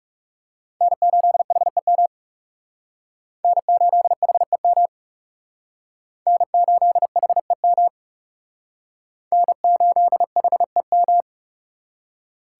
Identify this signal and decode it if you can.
Morse code, heard as a keyed tone, about 37 wpm, 700 Hz. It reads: N8HEM N8HEM N8HEM N8HEM
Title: Speed Racing - USA Call Signs 30wpm